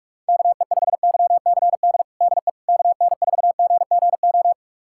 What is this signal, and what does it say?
Morse code, heard as a keyed tone, about 46 wpm, 700 Hz. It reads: KE5YCD DE KN4GGO
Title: USA Callsign DE Callsign - 6 Characters - 45wpm